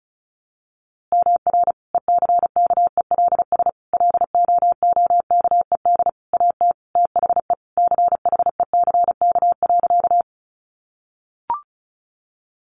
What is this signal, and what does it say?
Morse code, heard as a keyed tone, about 34 wpm, 700 Hz. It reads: MR ECKELS LOOKED AT THE CHECK.
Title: A Sound Of Thunder 35wpm